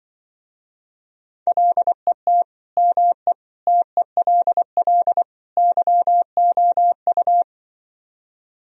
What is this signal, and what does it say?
Morse code, heard as a keyed tone, about 25 wpm, 700 Hz. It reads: LET ME TELL YOU
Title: Speed-Racing - Sets of 4 Words 20wpm